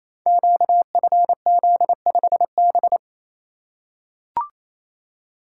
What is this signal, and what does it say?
Morse code, heard as a keyed tone, about 27 wpm, 700 Hz. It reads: QFZ5B
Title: Code Groups - 5 Letter:Numbers - 28wpm